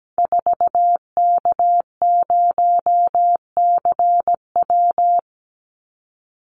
Morse code, heard as a keyed tone, about 17 wpm, 700 Hz. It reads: 4K0CW
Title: Call Signs Repeated 3x - 17wpm